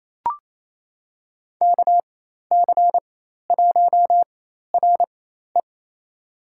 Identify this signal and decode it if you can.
Morse code, heard as a keyed tone, about 28 wpm, spaced as for 13 wpm, 700 Hz. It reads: KC1RE
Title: USA Call Signs - 4x Character Spacing - 28wpm